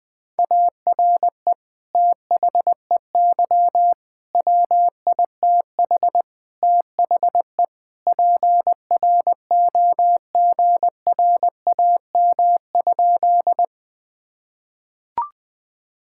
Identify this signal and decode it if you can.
Morse code, heard as a keyed tone, about 21 wpm, 700 Hz. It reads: ARE THEY WITH THE PROGRAM?
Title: Sentences from Top 1000 Words - Review Part 5 - 20wpm